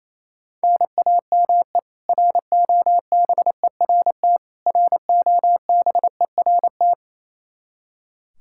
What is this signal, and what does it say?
Morse code, heard as a keyed tone, about 28 wpm, 700 Hz. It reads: NAME ROBERT ROBERT
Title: Practice QSO - Exchange 1 - Beginning - 28wpm